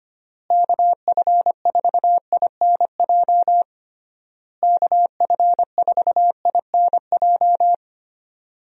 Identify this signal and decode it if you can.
Morse code, heard as a keyed tone, about 25 wpm, 700 Hz. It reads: KF4INJ KF4INJ
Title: USA Call Signs - 6 Characters - Repeated 3x - 25wpm